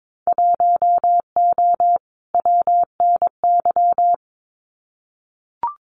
Morse code, heard as a keyed tone, about 22 wpm, 700 Hz. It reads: 1O WNY
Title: Winter Field Day Exchanges 22wpm